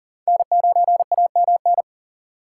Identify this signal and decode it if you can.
Morse code, heard as a keyed tone, about 40 wpm, 690 Hz. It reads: N9AMN
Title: USA Call Signs - 6x Character Spacing - 40wpm